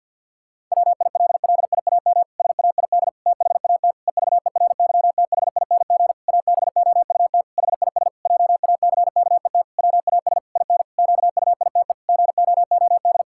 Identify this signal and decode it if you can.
Morse code, heard as a keyed tone, about 52 wpm, 700 Hz. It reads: WILLIAM SAID THAT EVERYTHING ABOUT HIS JACKET WAS IN QUITE GOOD
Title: Pangram Challenge v2 - 50wpm